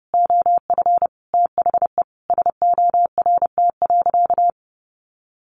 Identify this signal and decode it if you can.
Morse code, heard as a keyed tone, about 30 wpm, 700 Hz. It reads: OF THE SORT.
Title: Sentences from Top 1300 Words 30wpm